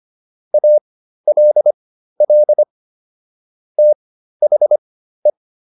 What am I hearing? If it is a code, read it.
Morse code, heard as a keyed tone, about 25 wpm, spaced as for 13 wpm, 600 Hz. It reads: ALL THE